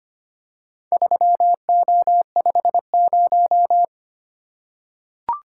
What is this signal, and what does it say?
Morse code, heard as a keyed tone, about 25 wpm, 700 Hz. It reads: 3O50
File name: practice / 25wpm / Four Letter-Number Combinations - 25wpm